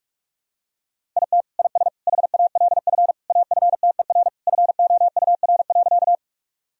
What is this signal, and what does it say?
Morse code, heard as a keyed tone, about 42 wpm, 700 Hz. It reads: IT IS HALF AFTER FOUR.
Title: Sentences from Top 1000 Words - Review Part 1 - 45wpm